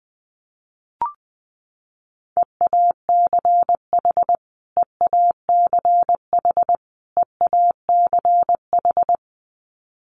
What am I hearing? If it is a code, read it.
Morse code, heard as a keyed tone, about 19 wpm, 700 Hz. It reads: EACH EACH EACH
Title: Top 200 Words Repeated 20wpm